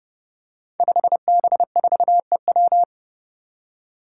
Morse code, heard as a keyed tone, about 30 wpm, 700 Hz. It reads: 5B4EW
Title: Five Challenging Letter-Number Combinations 30wpm